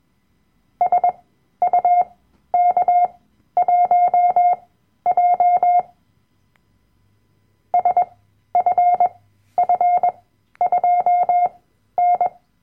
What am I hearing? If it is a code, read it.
Morse code, heard as a keyed tone, about 21 wpm, spaced as for 12 wpm, 710 Hz. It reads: SUK1J SFF2N
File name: Workshop Dateien / CW-Online-Kurs / Lektion 010